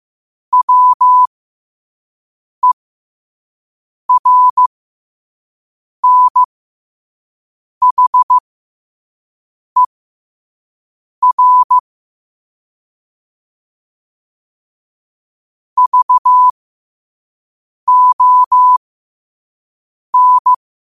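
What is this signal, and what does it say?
Morse code, heard as a keyed tone, about 15 wpm, spaced as for 5 wpm, 1000 Hz. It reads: WERNHER VON
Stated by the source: Morse code sent at 5 wpm.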